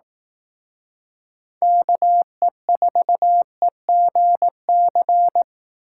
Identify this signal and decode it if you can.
Morse code, heard as a keyed tone, about 18 wpm, 700 Hz. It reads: KE4EGC
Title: Speed Racing - USA 2x3 Call Signs 15wpm